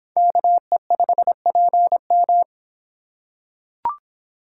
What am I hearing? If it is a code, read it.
Morse code, heard as a keyed tone, about 26 wpm, 700 Hz. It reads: KE5PM